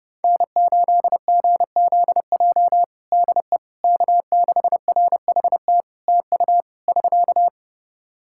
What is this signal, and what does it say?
Morse code, heard as a keyed tone, about 30 wpm, 700 Hz. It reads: N8GZJ DE K6RHT TU <SK>
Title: Practice QSO - Exchange 4 - Ending - 30wpm